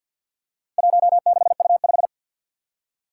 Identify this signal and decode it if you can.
Morse code, heard as a keyed tone, about 54 wpm, 700 Hz. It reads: 16U5
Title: Four Challenging Letter-Number Combinations 50wpm